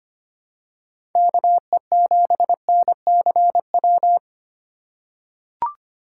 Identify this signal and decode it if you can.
Morse code, heard as a keyed tone, about 25 wpm, 700 Hz. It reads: KE7NCW
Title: USA Call Signs - 6 Characters - 4x Character Spacing - 25wpm